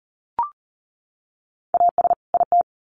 Morse code, heard as a keyed tone, about 40 wpm, 700 Hz. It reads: AS IT
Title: Sets of 2 Words 40wpm